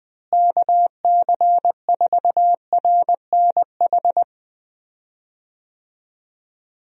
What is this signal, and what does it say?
Morse code, heard as a keyed tone, about 20 wpm, 700 Hz. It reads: KC4RNH
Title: USA Call Signs - 6 Characters - Repeated 3x - 20wpm